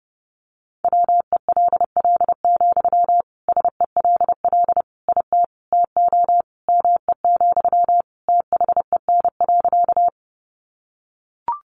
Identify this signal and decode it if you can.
Morse code, heard as a keyed tone, about 28 wpm, 700 Hz. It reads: WELL, SELL IT TO ME, THEN.
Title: Sentences from Top 1100 Words 30wpm